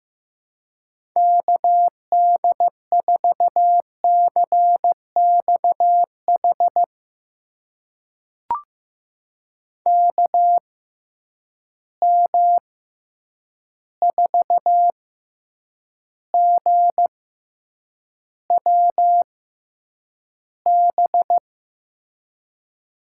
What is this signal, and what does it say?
Morse code, heard as a keyed tone, about 15 wpm, spaced as for 8 wpm, 700 Hz. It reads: KD4CXH KM4GWB
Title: USA Call Signs - 6 Characters - 6x Character Spacing - 15wpm